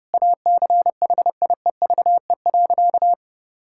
Morse code, heard as a keyed tone, about 32 wpm, 700 Hz. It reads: ACHIEVE.
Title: Sentences from Top 1000 Words - Review Part 6 - 30wpm